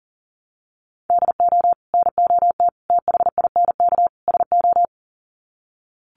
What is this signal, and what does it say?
Morse code, heard as a keyed tone, about 40 wpm, 700 Hz. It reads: DO NOT THINK SO
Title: Sets of 4 Words 40wpm